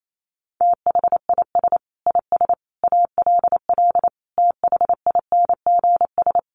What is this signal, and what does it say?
Morse code, heard as a keyed tone, about 28 wpm, 700 Hz. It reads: THIS IS ALL THINGS
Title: Sets of 5 Words 28wpm